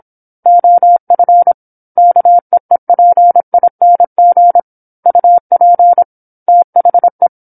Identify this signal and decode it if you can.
Morse code, heard as a keyed tone, about 27 wpm, 700 Hz. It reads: OF KEEPING UP THE
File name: 08-Chapter-08 26 WPM